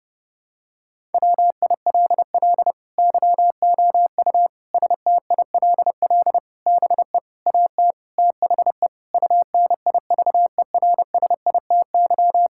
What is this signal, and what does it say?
Morse code, heard as a keyed tone, about 30 wpm, 700 Hz. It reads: WILL YOU STILL BE AT THE UNIVERSITY
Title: Sentences from Top 1000 Words - Review Part 5 - 30wpm